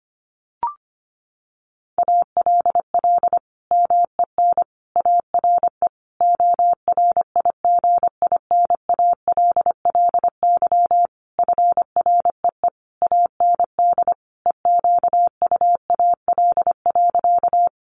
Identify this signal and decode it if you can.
Morse code, heard as a keyed tone, about 25 wpm, 700 Hz. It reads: ALL MEN ARE ORIGINALLY FREE AND EQUAL.
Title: Sentences from Top 1500 Words 25wpm